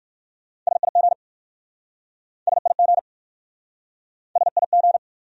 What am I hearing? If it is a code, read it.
Morse code, heard as a keyed tone, about 53 wpm, 700 Hz. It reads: SIG SIG SIG
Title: Speed-Racing Abbreviations - 45wpm